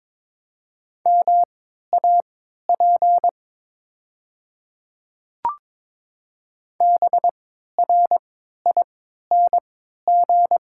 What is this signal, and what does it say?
Morse code, heard as a keyed tone, about 22 wpm, spaced as for 13 wpm, 690 Hz. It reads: MAP BRING
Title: 3x Character Spacing - Top 500 Words - 22wpm